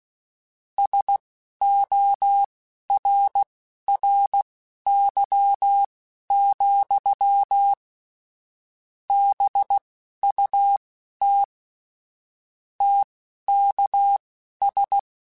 Morse code, heard as a keyed tone, about 16 wpm, spaced as for 11 wpm, 800 Hz. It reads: SORRY, BUT TKS